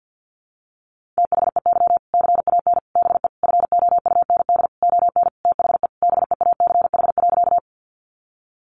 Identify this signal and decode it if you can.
Morse code, heard as a keyed tone, about 50 wpm, 700 Hz. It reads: THEY CAN BE FOUND ON THE BEACH.
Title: Sentences from Top 1500 Words 50wpm